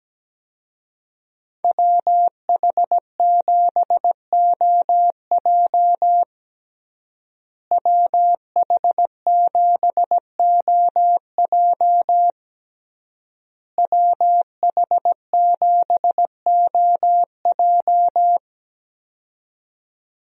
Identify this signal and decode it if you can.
Morse code, heard as a keyed tone, about 17 wpm, 700 Hz. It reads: WH7OJ WH7OJ WH7OJ
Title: USA Call Signs - 5 Characters - Repeated 3x - 17wpm